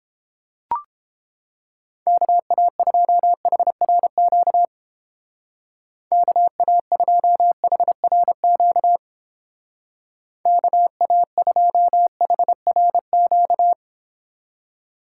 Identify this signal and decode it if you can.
Morse code, heard as a keyed tone, about 29 wpm, 700 Hz. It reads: KA2HRQ KA2HRQ KA2HRQ